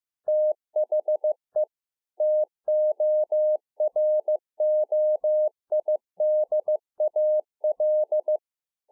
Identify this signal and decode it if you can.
Morse code, heard as a keyed tone, about 15 wpm, 610 Hz. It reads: THE TOROIDAL
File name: 2011-04-mp3 / A SMALL TRANCEIVER 2 15WPM